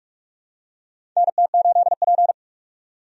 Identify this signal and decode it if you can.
Morse code, heard as a keyed tone, about 46 wpm, 700 Hz. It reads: NT8P
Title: USA Call Signs - 4 Characters - 45wpm